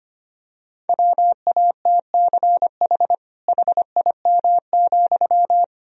Morse code, heard as a keyed tone, about 25 wpm, 700 Hz. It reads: WATCH HIM,
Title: Sentences from Top 300 Words and Abbreviations 25wpm